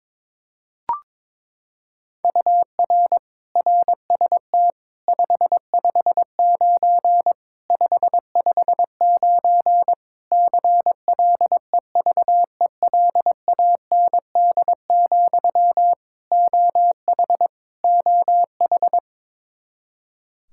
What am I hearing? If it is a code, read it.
Morse code, heard as a keyed tone, about 22 wpm, 700 Hz. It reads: UR RST 559 559 CLEVELAND, OH OH
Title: Practice QSO - Exchange 1 - Beginning - 22wpm